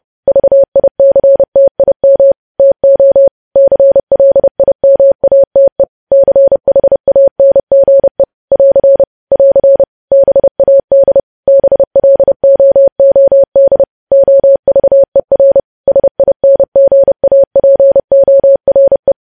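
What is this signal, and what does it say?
Morse code, heard as a keyed tone, about 30 wpm, 550 Hz. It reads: VICTIM TO CLIMATE CHANGE <AR> <AR> BAD BLOOD OVER SINGAPORE
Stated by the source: Morse code practice of news headlines at 30.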